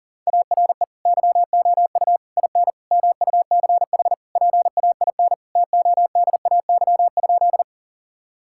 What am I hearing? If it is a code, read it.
Morse code, heard as a keyed tone, about 40 wpm, 700 Hz. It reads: ARE YOU IN MUCH PAIN TODAY?
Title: Sentences from Top 1600 Words 40wpm